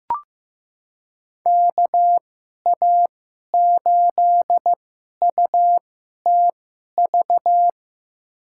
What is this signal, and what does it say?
Morse code, heard as a keyed tone, about 15 wpm, spaced as for 11 wpm, 700 Hz. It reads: KA8UTV
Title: USA Call Signs - 6 Characters - 2x Character Spacing - 15wpm